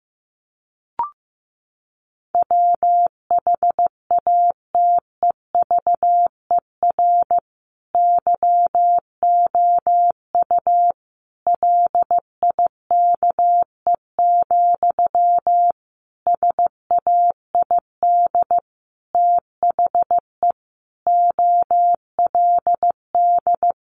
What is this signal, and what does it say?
Morse code, heard as a keyed tone, about 15 wpm, 700 Hz. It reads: WHATEVER YOU LIKE, SAID THE OLD
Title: Sentences from Top 1100 Words 15wpm